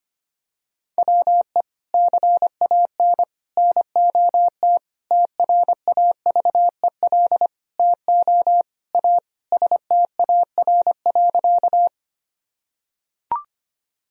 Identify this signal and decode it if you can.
Morse code, heard as a keyed tone, about 25 wpm, 700 Hz. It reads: WE CAN NOT TRAVEL TO A STAR.
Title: Sentences from Top 1000 Words - Review Part 2 - 25wpm